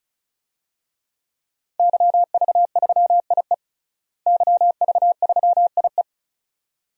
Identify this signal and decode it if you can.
Morse code, heard as a keyed tone, about 34 wpm, 700 Hz. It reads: YV3IE YV3IE
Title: DXCC-100 Call Signs - 5 Characters - Repeated 2x - 35wpm